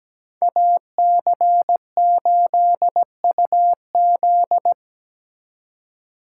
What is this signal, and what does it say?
Morse code, heard as a keyed tone, about 20 wpm, 700 Hz. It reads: AC8UZ
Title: Speed Racing - USA 2x2 Call Signs 17wpm